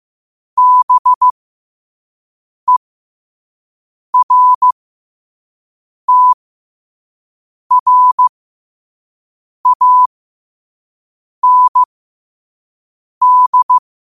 Morse code, heard as a keyed tone, about 15 wpm, spaced as for 5 wpm, 1000 Hz. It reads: BERTRAND
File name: qotdfiles / 5wpm